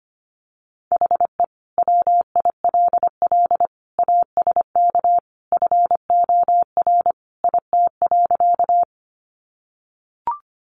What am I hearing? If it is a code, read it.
Morse code, heard as a keyed tone, about 25 wpm, 700 Hz. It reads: HE WILL ASK FOR IT.
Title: Sentences from Top 200 Words 25wpm